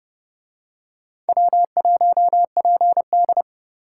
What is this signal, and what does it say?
Morse code, heard as a keyed tone, about 30 wpm, 700 Hz. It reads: W1PD